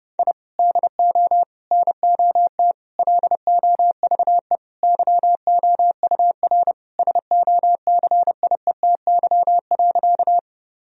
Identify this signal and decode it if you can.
Morse code, heard as a keyed tone, about 30 wpm, 700 Hz. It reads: I DO NOT LOVE YOUR SOCIETY.
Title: Sentences from Top 1000 Words - Review Part 4 - 30wpm